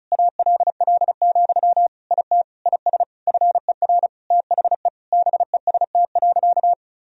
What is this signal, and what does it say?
Morse code, heard as a keyed tone, about 34 wpm, 700 Hz. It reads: ALL, IT IS FER THE BEST.
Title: Sentences from Top 500 Words and Abbreviations 35wpm